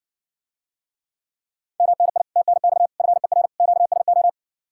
Morse code, heard as a keyed tone, about 61 wpm, 700 Hz. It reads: MNI TNX FER QSO